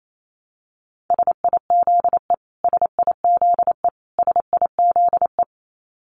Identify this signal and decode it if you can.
Morse code, heard as a keyed tone, about 28 wpm, 700 Hz. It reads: SIZE SIZE SIZE
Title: Top 500 Words Repeated 28wpm